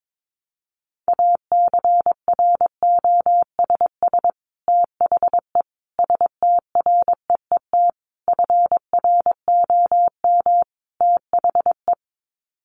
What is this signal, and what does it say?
Morse code, heard as a keyed tone, about 22 wpm, 700 Hz. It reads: ACROSS THE STREET FROM THE
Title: Sets of 5 Words Encore 22wpm